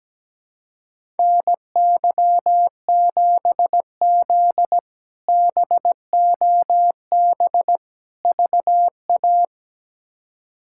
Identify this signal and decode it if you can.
Morse code, heard as a keyed tone, about 17 wpm, 700 Hz. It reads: NY7Z BOB VA
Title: K1USN SST QSO Practice - 17wpm